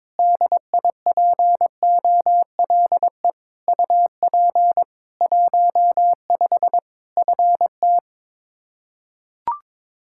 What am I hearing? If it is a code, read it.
Morse code, heard as a keyed tone, about 22 wpm, 700 Hz. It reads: DIPOLE UP 15 FT
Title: Practice QSO - Exchange 2 - Info - 22wpm